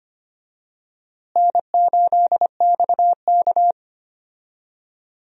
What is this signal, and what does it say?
Morse code, heard as a keyed tone, about 25 wpm, 700 Hz. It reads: N8XK
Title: USA Call Signs - 4 Characters - Repeated 2x - 25wpm